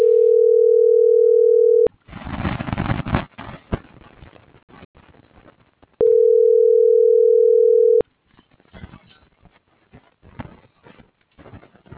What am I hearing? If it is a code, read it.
Ringback tone (North America), 440 + 480 Hz.